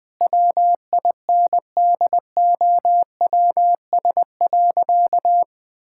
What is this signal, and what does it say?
Morse code, heard as a keyed tone, about 20 wpm, 700 Hz. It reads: WINDOWS.
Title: Pangram Challenge v2 - 20wpm